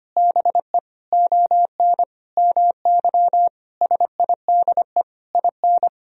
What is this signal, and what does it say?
Morse code, heard as a keyed tone, about 25 wpm, 700 Hz. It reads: BE ON MY SIDE IN
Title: Sentences from Top 1000 Words - Review Part 1 - 25wpm